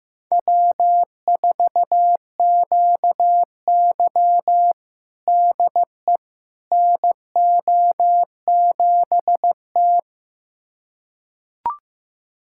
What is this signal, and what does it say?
Morse code, heard as a keyed tone, about 15 wpm, 700 Hz. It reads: W4QY DE NO7T
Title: USA Callsign DE Callsign - 4 Characters - 15wpm